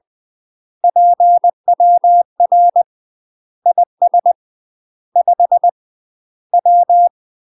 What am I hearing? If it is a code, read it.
Morse code, heard as a keyed tone, about 20 wpm, 700 Hz. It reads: PWR IS 5 W
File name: UK MASTER QSOs TRAINING / 20WPM / UK MASTER QSOs 0050